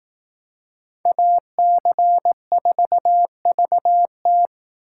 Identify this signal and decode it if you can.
Morse code, heard as a keyed tone, about 19 wpm, 700 Hz. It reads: AC4VT
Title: Speed Racing - USA 2x2 Call Signs 15wpm